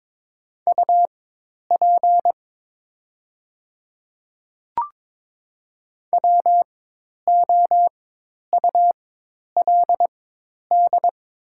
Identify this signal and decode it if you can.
Morse code, heard as a keyed tone, about 22 wpm, spaced as for 10 wpm, 700 Hz. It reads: UP WOULD
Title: 4x Character Spacing - Top 200 Words - 22wpm